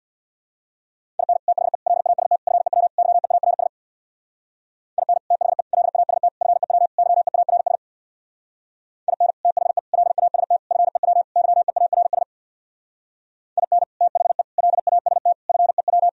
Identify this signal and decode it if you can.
Morse code, heard as a keyed tone, about 64 wpm, 690 Hz. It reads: IN THE LAST FEW YEARS IN THE LAST FEW YEARS IN THE LAST FEW YEARS IN THE LAST FEW